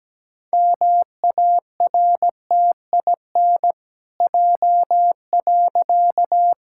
Morse code, heard as a keyed tone, about 17 wpm, 700 Hz. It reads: MARTIN J.
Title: Pangram Challenge v2 - 17wpm